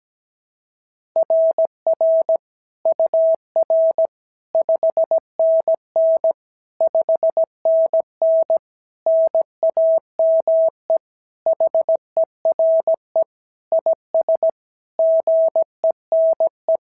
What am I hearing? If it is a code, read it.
Morse code, heard as a keyed tone, about 16 wpm, 650 Hz. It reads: RR UR 5NN 5NN NAME HERE IS GENE
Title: SOTA QSO Practice 17wpm